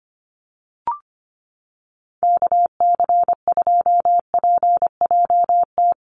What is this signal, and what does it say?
Morse code, heard as a keyed tone, about 25 wpm, 700 Hz. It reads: KC2PJT
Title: USA Call Signs Repeated 2x 25wpm